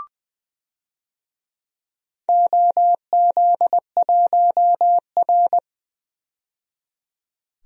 Morse code, heard as a keyed tone, about 20 wpm, 700 Hz. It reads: OZ1R
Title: DXCC-100 Call Signs - 4 Characters - 20wpm